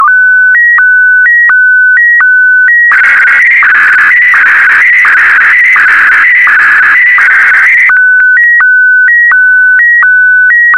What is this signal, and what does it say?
SSTV (slow-scan television), PD 90.
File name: sstv / W2180-VK4MSL